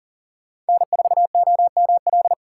Morse code, heard as a keyed tone, about 40 wpm, 700 Hz. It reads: N4OML
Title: USA Call Signs - 5 Characters - 4x Character Spacing - 40wpm